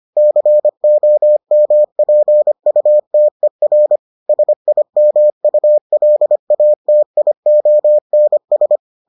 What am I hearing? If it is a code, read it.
Morse code, heard as a keyed tone, about 25 wpm, 600 Hz. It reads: COMPUTER SIMULATIONS